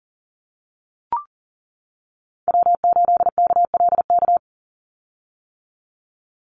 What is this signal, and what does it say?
Morse code, heard as a keyed tone, about 40 wpm, 700 Hz. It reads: W8KLK
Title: USA Call Signs 40wpm